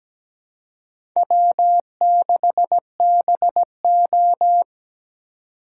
Morse code, heard as a keyed tone, about 20 wpm, 700 Hz. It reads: W6BO
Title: Speed Racing - USA 1x2 Call Signs 17wpm